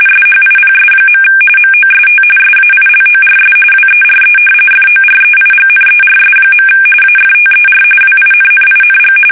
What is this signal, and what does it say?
RTTY, about 45.45 baud. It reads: SHIPS. MOST SAILORS WERE FOREIGNERS. AFTER WINNING
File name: fsk850 / usnavy1